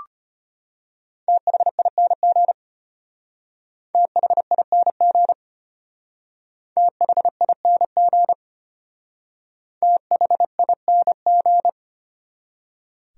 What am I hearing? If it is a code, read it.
Morse code, heard as a keyed tone, about 32 wpm, 700 Hz. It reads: THING THING THING THING